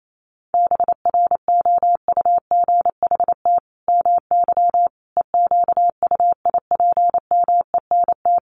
Morse code, heard as a keyed tone, about 28 wpm, 700 Hz. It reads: BROUGHT MY EQUIPMENT
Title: Sentences from Top 700 Words 28wpm